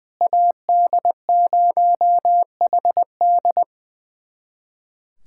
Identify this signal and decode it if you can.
Morse code, heard as a keyed tone, about 21 wpm, 690 Hz. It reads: AD0HD
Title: SOTA QSO Practice 20wpm